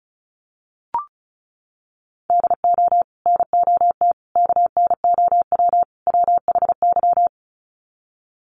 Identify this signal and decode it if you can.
Morse code, heard as a keyed tone, about 34 wpm, 700 Hz. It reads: DO NOT KNOW WHY
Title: Sets of 4 Words 35wpm